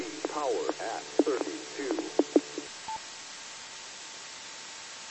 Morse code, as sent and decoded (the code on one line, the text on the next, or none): none